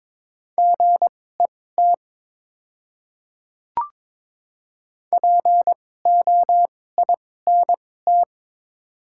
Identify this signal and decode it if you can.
Morse code, heard as a keyed tone, about 22 wpm, spaced as for 16 wpm, 700 Hz. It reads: GET POINT
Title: 2x Character Spacing - Top 200 Words - 22wpm